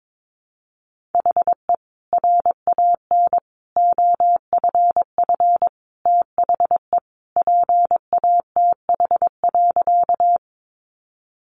Morse code, heard as a keyed tone, about 22 wpm, 700 Hz. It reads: HE RAN OFF THE PATH.
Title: A Sound Of Thunder 22wpm